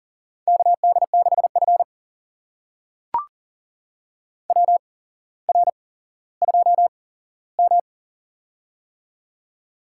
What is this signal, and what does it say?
Morse code, heard as a keyed tone, about 40 wpm, spaced as for 17 wpm, 700 Hz. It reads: KD6F WR2M
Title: USA Call Signs - 4 Characters - 8x Character Spacing - 40wpm